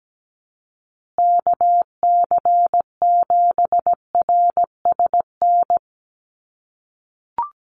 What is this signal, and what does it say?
Morse code, heard as a keyed tone, about 17 wpm, 700 Hz. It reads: KC7RSN
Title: USA Call Signs Repeated 2x 17wpm